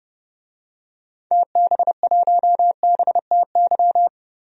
Morse code, heard as a keyed tone, about 32 wpm, 700 Hz. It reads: TB1BTY
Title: DXCC-100 Call Signs - 6 Characters - 30wpm